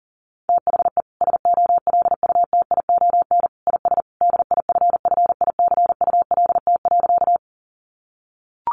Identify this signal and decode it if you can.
Morse code, heard as a keyed tone, about 40 wpm, 700 Hz. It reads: THE SOLUTION IS DIFFICULT.
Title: Sentences from Top 1300 Words 40wpm